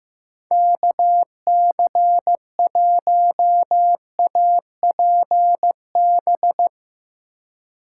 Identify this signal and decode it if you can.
Morse code, heard as a keyed tone, about 15 wpm, 700 Hz. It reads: KC1APB
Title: USA Call Signs - 6 Characters - 2x Character Spacing - 15wpm